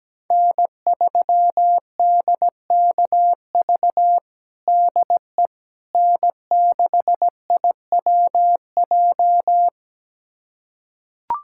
Morse code, heard as a keyed tone, about 17 wpm, 700 Hz. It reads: N3DKV DE N6IWJ
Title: USA Callsign DE Callsign - 5 Characters - 17wpm